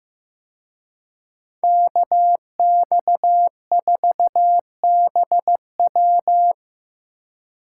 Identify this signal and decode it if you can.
Morse code, heard as a keyed tone, about 18 wpm, 700 Hz. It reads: KX4BW
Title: Speed Racing - USA Call Signs 15wpm